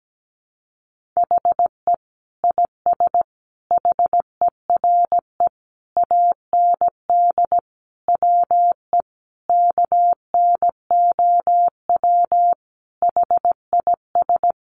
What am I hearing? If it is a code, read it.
Morse code, heard as a keyed tone, about 17 wpm, 700 Hz. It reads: HE IS HERE AND WE KNOW HIS
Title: Sentences from Top 800 Words 17wpm